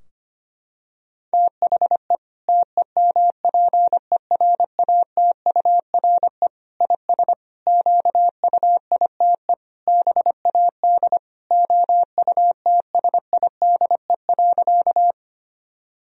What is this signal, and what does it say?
Morse code, heard as a keyed tone, about 25 wpm, 700 Hz. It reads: THE TEMPERATURE IS QUITE BAD OUTSIDE.
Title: Sentences from Top 1000 Words - Review Part 4 - 25wpm